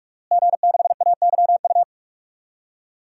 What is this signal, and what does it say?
Morse code, heard as a keyed tone, about 46 wpm, 700 Hz. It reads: G6AYU
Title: DXCC-100 Call Signs - 5 Characters - Repeated 3x - 45wpm